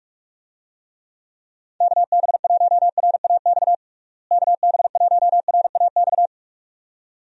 Morse code, heard as a keyed tone, about 43 wpm, 700 Hz. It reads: KB1RAX KB1RAX
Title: USA Call Signs - 6 Characters - Repeated 2x - 45wpm